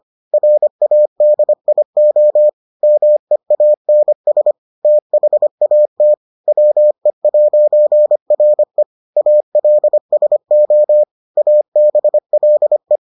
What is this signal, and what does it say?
Morse code, heard as a keyed tone, about 25 wpm, 600 Hz. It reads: RADIO MEANS THAT WE'RE ALSO ABLE